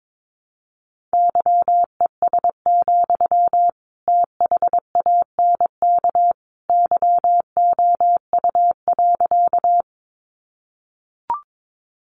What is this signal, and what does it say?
Morse code, heard as a keyed tone, about 22 wpm, 700 Hz. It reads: YES, THANK YOU.
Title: Sentences from Top 600 Words 22wpm